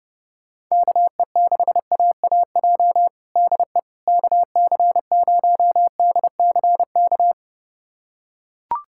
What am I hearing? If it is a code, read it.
Morse code, heard as a keyed tone, about 30 wpm, 700 Hz. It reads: KE6AAJ DE KC0DCK
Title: USA Callsign DE Callsign - 6 Characters - 30wpm